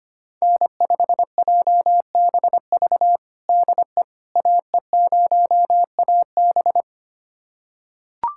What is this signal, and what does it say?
Morse code, heard as a keyed tone, about 25 wpm, 700 Hz. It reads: N5JBV DE AE0AB
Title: USA Callsign DE Callsign - 5 Characters - 25wpm